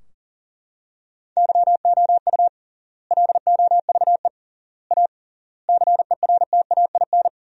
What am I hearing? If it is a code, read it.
Morse code, heard as a keyed tone, about 40 wpm, 700 Hz. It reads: YOU LOVE A CERTAIN
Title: 3x Word-Spacing - Sentences from Top 500 Words - 40wpm